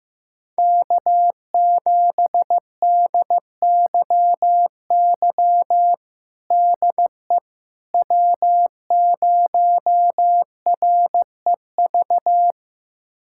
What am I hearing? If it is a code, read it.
Morse code, heard as a keyed tone, about 15 wpm, 700 Hz. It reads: K7DYY DE W0REV
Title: USA Callsign DE Callsign - 5 Characters - 15wpm